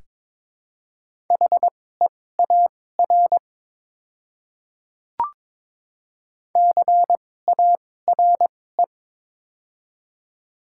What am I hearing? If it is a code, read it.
Morse code, heard as a keyed tone, about 22 wpm, spaced as for 16 wpm, 700 Hz. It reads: HEAR CARE
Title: 2x Character Spacing - Top 500 Words - 22wpm